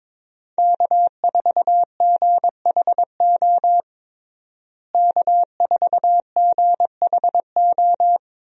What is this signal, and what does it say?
Morse code, heard as a keyed tone, about 22 wpm, 700 Hz. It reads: K4GHO K4GHO
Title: USA Call Signs - 5 Characters - Repeated 2x - 22wpm